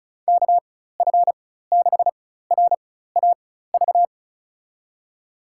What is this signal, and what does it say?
Morse code, heard as a keyed tone, about 34 wpm, spaced as for 16 wpm, 700 Hz. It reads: KF6RAV
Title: USA Call Signs - 6 Characters - 4x Character Spacing - 35wpm